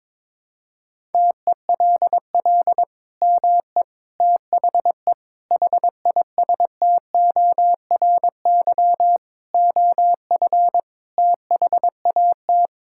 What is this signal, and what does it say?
Morse code, heard as a keyed tone, about 22 wpm, 700 Hz. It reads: TELL ME THE HISTORY OF THAT
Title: Sentences from Top 1000 Words - Review Part 5 - 22wpm